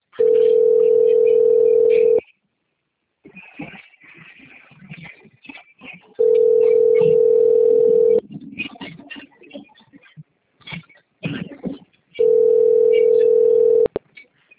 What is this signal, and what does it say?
Ringback tone (North America), 440 + 480 Hz.